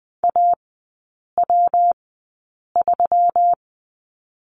Morse code, heard as a keyed tone, about 20 wpm, spaced as for 8 wpm, 690 Hz. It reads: AW3
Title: 133 Lesson 22e - Question Mark - Three Characters 20wpm